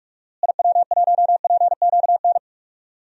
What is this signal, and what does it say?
Morse code, heard as a keyed tone, about 46 wpm, 700 Hz. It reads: IW1PQN
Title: DXCC-100 Call Signs - 6 Characters - Repeated 2x - 45wpm